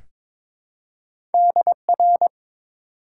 Morse code, heard as a keyed tone, about 22 wpm, 700 Hz. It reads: DR
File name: Top 90 QSO Elements - 4x Character Spacing - 22wpm